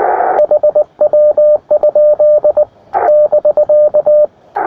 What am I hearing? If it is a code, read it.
Morse code, heard as a keyed tone, about 19 wpm, 600 Hz, with radio noise. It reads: HW? <BK>
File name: attachments / iz0aym 20m